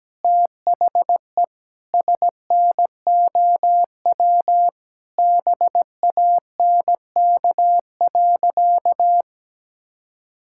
Morse code, heard as a keyed tone, about 17 wpm, 700 Hz. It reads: THE SNOW BANK.
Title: Sentences from Top 1600 Words 17wpm